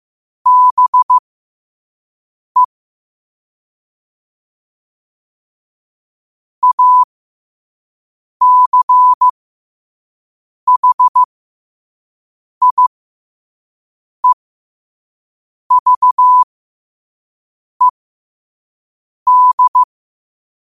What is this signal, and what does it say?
Morse code, heard as a keyed tone, about 15 wpm, spaced as for 5 wpm, 1000 Hz. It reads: BE ACHIEVED
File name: qotdfiles / 5wpm